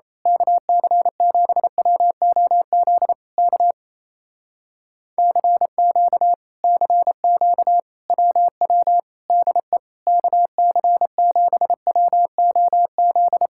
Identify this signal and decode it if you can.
Morse code, heard as a keyed tone, about 27 wpm, 700 Hz. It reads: KC7WOZ K CQ CQ WW DE KC7WOZ
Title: Speed-Racing Common QSO Phrases Encore - 28wpm